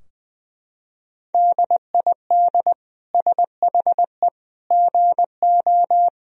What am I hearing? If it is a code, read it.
Morse code, heard as a keyed tone, about 21 wpm, 700 Hz. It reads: DID SHE GO
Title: Sentences from Top 1000 Words - Review Part 3 - 20wpm